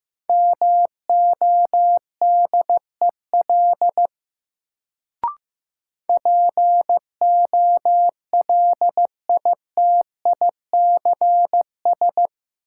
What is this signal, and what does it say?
Morse code, heard as a keyed tone, about 15 wpm, 700 Hz. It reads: MODEL POLITICS
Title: Sentences from Top 1000 Words - New Word Drill - 15wpm